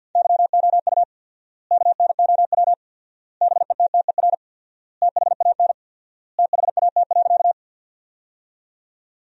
Morse code, heard as a keyed tone, about 50 wpm, 700 Hz. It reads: YOU KNOW BETTER THAN THAT.
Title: 4x Word-Spacing - Sentences from Top 400 Words - 50wpm